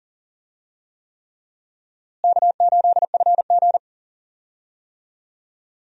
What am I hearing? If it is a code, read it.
Morse code, heard as a keyed tone, about 40 wpm, 700 Hz. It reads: K8FG
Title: USA Call Signs - 4 Characters - Repeated 3x - 40wpm